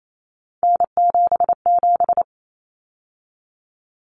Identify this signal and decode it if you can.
Morse code, heard as a keyed tone, about 28 wpm, 700 Hz. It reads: N77
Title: Common Cut Numbers - 3-Digit 28wpm